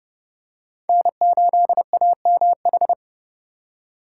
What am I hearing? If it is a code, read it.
Morse code, heard as a keyed tone, about 29 wpm, 700 Hz. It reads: N8AMH
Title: USA Call Signs - 5 Characters - 2x Character Spacing - 30wpm